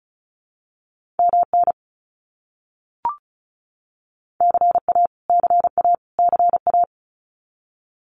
Morse code, heard as a keyed tone, about 34 wpm, 700 Hz. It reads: MN CA CA CA
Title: US States Repeated 3x 35wpm